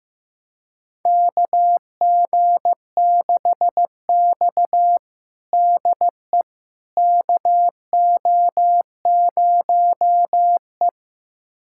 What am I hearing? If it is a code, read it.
Morse code, heard as a keyed tone, about 15 wpm, 700 Hz. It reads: KG6X DE KO0E
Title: USA Callsign DE Callsign - 4 Characters - 15wpm